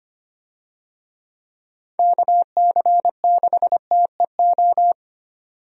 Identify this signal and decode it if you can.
Morse code, heard as a keyed tone, about 25 wpm, 700 Hz. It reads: KC6TEO
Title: USA Call Signs - 6 Characters - 25wpm